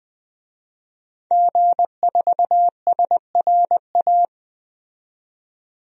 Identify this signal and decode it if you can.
Morse code, heard as a keyed tone, about 20 wpm, 700 Hz. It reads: G4SRA
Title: DXCC-100 Call Signs - 5 Characters - 20wpm